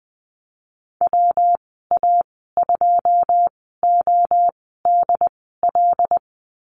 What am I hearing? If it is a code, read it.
Morse code, heard as a keyed tone, about 20 wpm, spaced as for 15 wpm, 700 Hz. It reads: WA2ODL
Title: 154 Lesson 25g - Letter p - Call Signs 20wpm